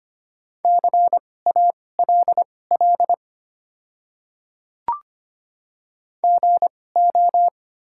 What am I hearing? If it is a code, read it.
Morse code, heard as a keyed tone, about 25 wpm, spaced as for 18 wpm, 690 Hz. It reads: CALL GO
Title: 2x Character Spacing - Top 100 Words - 25wpm